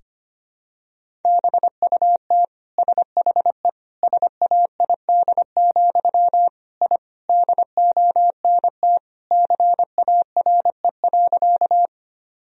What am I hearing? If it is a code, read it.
Morse code, heard as a keyed tone, about 25 wpm, 700 Hz. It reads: BUT SHE SAID, I DONT CARE.
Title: Sentences from Top 1000 Words - Review Part 1 - 25wpm